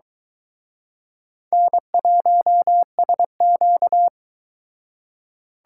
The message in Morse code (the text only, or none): N1SQ